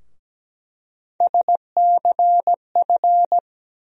Morse code, heard as a keyed tone, about 17 wpm, 700 Hz. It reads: SCF
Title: USA Call Sign Postfixes - 3 Letters 17wpm